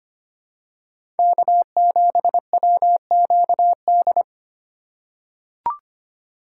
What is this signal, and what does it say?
Morse code, heard as a keyed tone, about 25 wpm, 700 Hz. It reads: K7WQD
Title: USA Call Signs - 5 Characters - 3x Character Spacing - 25wpm